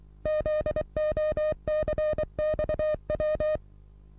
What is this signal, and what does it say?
Morse code, heard as a keyed tone, about 23 wpm, 610 Hz. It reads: ZOCXW
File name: repos / Morsekurs / DJ4UF Morsen27